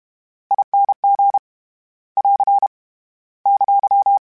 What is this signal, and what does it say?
Morse code, heard as a keyed tone, about 32 wpm, 800 Hz. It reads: I N G <AR> !
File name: Special INGs / Special INGs-mp3s / VerySlowToFaster Ing @30